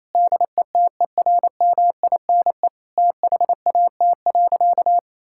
Morse code, heard as a keyed tone, about 28 wpm, 700 Hz. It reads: DETERMINE THAT.
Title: Sentences from Top 1000 Words - Review Part 5 - 28wpm